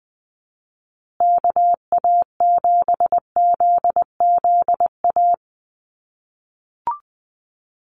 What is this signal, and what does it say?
Morse code, heard as a keyed tone, about 20 wpm, 700 Hz. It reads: KA7ZZA
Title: USA Call Signs Repeated 3x 20wpm